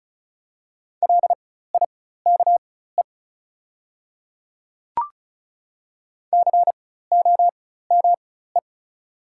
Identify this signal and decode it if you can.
Morse code, heard as a keyed tone, about 33 wpm, spaced as for 16 wpm, 690 Hz. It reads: LIKE COME
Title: 4x Character Spacing - Top 100 Words - 35wpm